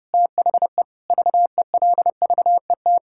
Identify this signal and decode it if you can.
Morse code, heard as a keyed tone, about 32 wpm, 700 Hz. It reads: THE VELVET
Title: Pangram Challenge v2 - 30wpm